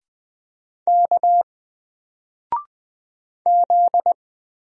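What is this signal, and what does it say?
Morse code, heard as a keyed tone, about 20 wpm, 700 Hz. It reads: K Z
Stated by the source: Single letter and number Morse drill at 20 wpm, from N4LCD.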